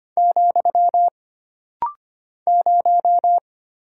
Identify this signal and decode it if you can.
Morse code, heard as a keyed tone, about 25 wpm, 700 Hz. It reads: , 0
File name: Single Character (Letter:Number:Punctuation:Prosign) - Rapid-Fire - 25wpm